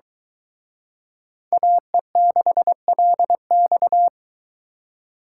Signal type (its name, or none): Morse code